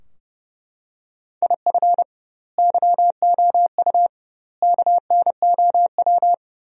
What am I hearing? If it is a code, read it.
Morse code, heard as a keyed tone, about 30 wpm, 700 Hz. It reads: IF YOU KNOW
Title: 2x Word-Spacing - Sentences from Top 400 Words - 30wpm